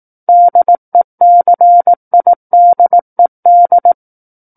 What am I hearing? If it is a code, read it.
Morse code, heard as a keyed tone, about 18 wpm, 700 Hz. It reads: DECIDED
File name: 18 WPM